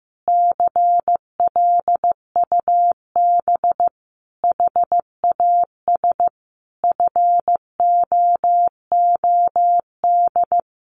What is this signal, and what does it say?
Morse code, heard as a keyed tone, about 15 wpm, 700 Hz. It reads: CLUB HAS FOOD
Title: Sentences from Top 1200 Words 15wpm